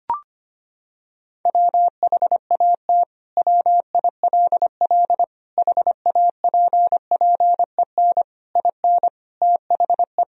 Morse code, heard as a keyed tone, about 25 wpm, 700 Hz. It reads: WHAT WILL HAPPEN IN THE
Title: Sentences from Top 1000 Words - Review Part 2 - 25wpm